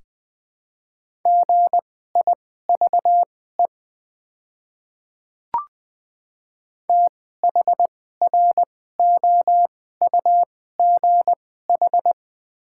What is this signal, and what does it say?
Morse code, heard as a keyed tone, about 20 wpm, spaced as for 14 wpm, 700 Hz. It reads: GIVE THROUGH
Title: 2x Character Spacing - Top 200 Words - 20wpm